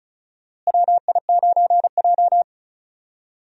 Morse code, heard as a keyed tone, about 34 wpm, 700 Hz. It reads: WI9J
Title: USA Call Signs - 4 Characters - Repeated 3x - 35wpm